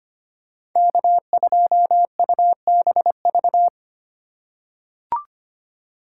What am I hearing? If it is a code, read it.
Morse code, heard as a keyed tone, about 25 wpm, 700 Hz. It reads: K2UBV
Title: USA Call Signs - 5 Characters - 3x Character Spacing - 25wpm